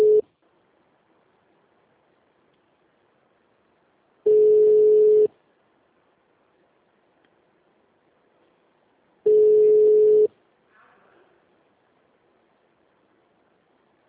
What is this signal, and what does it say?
Ringback tone (Europe, 425 Hz).
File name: out- -2014- - - .15488